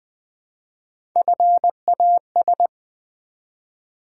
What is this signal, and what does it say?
Morse code, heard as a keyed tone, about 20 wpm, 700 Hz. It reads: FAS
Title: Build A Word - 1 Letter At A Time - 3 to 5-Letter Words - 20wpm